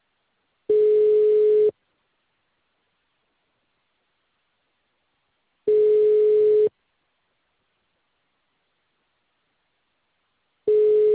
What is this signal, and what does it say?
Ringback tone (Europe, 425 Hz).